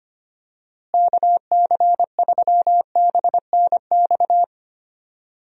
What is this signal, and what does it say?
Morse code, heard as a keyed tone, about 25 wpm, 700 Hz. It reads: KC3BNX
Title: USA Call Signs - 6 Characters - Repeated 2x - 25wpm